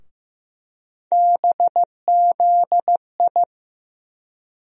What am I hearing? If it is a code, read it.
Morse code, heard as a keyed tone, about 15 wpm, 700 Hz. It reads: BZI
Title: USA Call Sign Postfixes - 3 Letters 15wpm